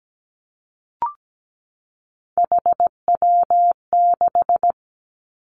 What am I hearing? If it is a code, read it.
Morse code, heard as a keyed tone, about 17 wpm, 700 Hz. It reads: HW6
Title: Three Challenging Letter-Number Combinations 17wpm